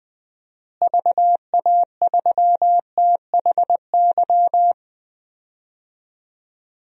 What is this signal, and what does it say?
Morse code, heard as a keyed tone, about 21 wpm, 700 Hz. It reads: VA3THY
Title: DXCC-100 Call Signs - 6 Characters - Repeated 3x - 20wpm